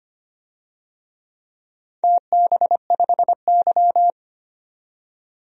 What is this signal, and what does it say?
Morse code, heard as a keyed tone, about 25 wpm, 690 Hz. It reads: TB5Y
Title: DXCC-100 Call Signs - 4 Characters - 25wpm